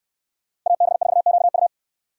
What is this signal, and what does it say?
Morse code, heard as a keyed tone, about 69 wpm, 700 Hz. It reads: AB4QU